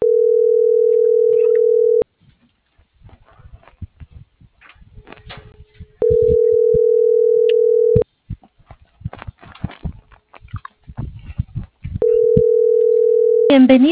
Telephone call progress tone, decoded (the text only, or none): ringback tone